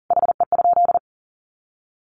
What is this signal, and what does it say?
Morse code, heard as a keyed tone, about 40 wpm, 700 Hz. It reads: HE?
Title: Sentences from Top 1300 Words 40wpm